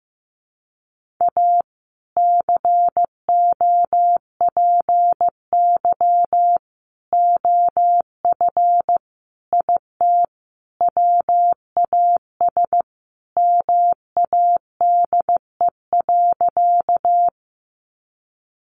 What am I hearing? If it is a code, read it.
Morse code, heard as a keyed tone, about 15 wpm, 700 Hz. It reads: A COPY OF IT WAS MADE.
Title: Sentences from Top 1400 Words 15wpm